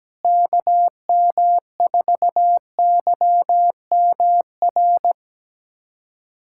Morse code, heard as a keyed tone, about 17 wpm, 700 Hz. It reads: KM4YMR
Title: USA Callsign DE Callsign - 6 Characters - 17wpm